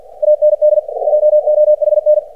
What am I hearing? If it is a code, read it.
Morse code, heard as a keyed tone, about 50 wpm, 600 Hz, with radio noise. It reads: AND JOHN
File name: archived / CRSnet